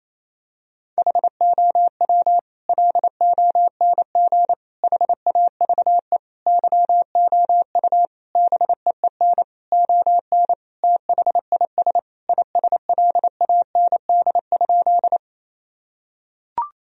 Morse code, heard as a keyed tone, about 27 wpm, 700 Hz. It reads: HOW LONG HAVE YOU BEEN ON THIS ISLAND?
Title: Sentences from Top 1000 Words - Review Part 3 - 28wpm